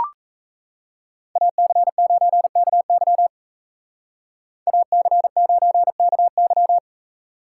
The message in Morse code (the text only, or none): AC9KY AC9KY